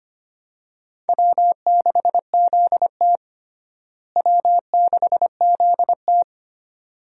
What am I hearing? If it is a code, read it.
Morse code, heard as a keyed tone, about 25 wpm, 700 Hz. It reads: W6ZT W6ZT
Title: USA Call Signs - 4 Characters - Repeated 2x - 25wpm